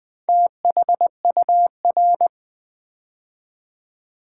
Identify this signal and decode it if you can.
Morse code, heard as a keyed tone, about 20 wpm, 700 Hz. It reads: THUR
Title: Build A Word - 4 Letters At A Time - 8-Letter Words - 20wpm